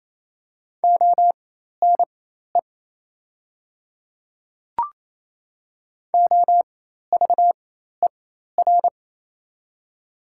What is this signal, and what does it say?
Morse code, heard as a keyed tone, about 28 wpm, spaced as for 13 wpm, 700 Hz. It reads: ONE OVER
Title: 4x Character Spacing - Top 100 Words - 28wpm